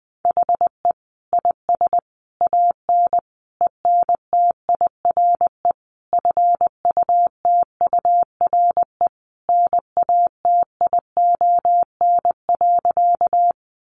Morse code, heard as a keyed tone, about 19 wpm, 700 Hz. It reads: HE IS AN ENTIRE FUTURE NATION.
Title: A Sound Of Thunder 20wpm